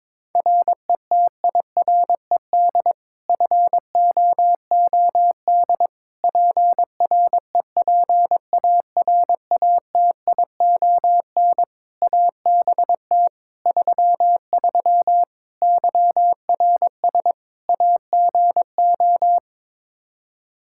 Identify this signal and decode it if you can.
Morse code, heard as a keyed tone, about 22 wpm, 700 Hz. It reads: RETIRED FOOD PREPARATION ABT 33 YRS AGO
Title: K1USN SST QSO Practice 22wpm